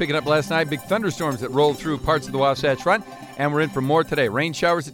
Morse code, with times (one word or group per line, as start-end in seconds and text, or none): none